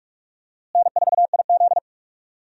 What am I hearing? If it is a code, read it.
Morse code, heard as a keyed tone, about 43 wpm, 700 Hz. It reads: N4IZ
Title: USA Call Signs - 4 Characters - Repeated 2x - 45wpm